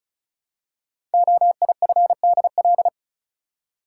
Morse code, heard as a keyed tone, about 36 wpm, 700 Hz. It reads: OIFDL
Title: Code Groups - 5 Letters - 35wpm